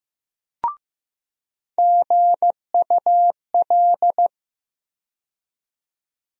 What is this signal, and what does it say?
Morse code, heard as a keyed tone, about 15 wpm, 700 Hz. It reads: GUL
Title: USA Call Sign Postfixes - 3 Letters 15wpm